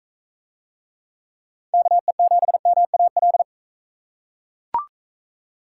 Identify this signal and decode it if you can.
Morse code, heard as a keyed tone, about 38 wpm, 700 Hz. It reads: KE7MAL